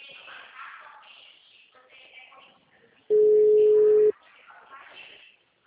Ringback tone (Europe, 425 Hz).